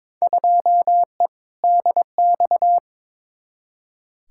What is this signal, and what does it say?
Morse code, heard as a keyed tone, about 22 wpm, 700 Hz. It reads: 2E DX
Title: Field Day Exchanges - 22wpm